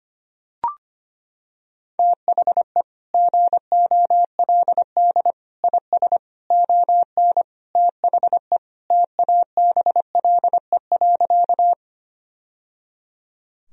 Morse code, heard as a keyed tone, about 25 wpm, 700 Hz. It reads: THE GOLD IS ON THE TABLE.
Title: Sentences from Top 1000 Words - Review Part 3 - 25wpm